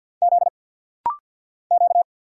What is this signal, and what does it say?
Morse code, heard as a keyed tone, about 50 wpm, 690 Hz. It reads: Z Q
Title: Single Letter-Number Alt - Rapid-Fire 50wpm